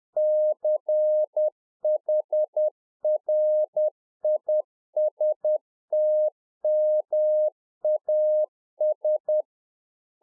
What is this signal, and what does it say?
Morse code, heard as a keyed tone, about 10 wpm, 610 Hz. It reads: CHRISTMAS